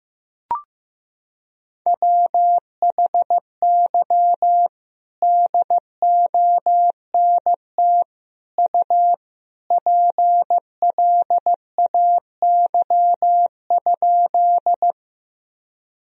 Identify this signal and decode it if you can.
Morse code, heard as a keyed tone, about 15 wpm, 700 Hz. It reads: WHY DONT U PLAY?
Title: Sentences from Top 300 Words and Abbreviations 15wpm